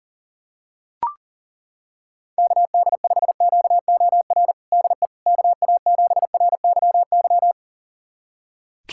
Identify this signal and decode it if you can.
Morse code, heard as a keyed tone, about 40 wpm, 700 Hz. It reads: KD5QOR DE KA7RYY
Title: USA Callsign DE Callsign - 6 Characters - 40wpm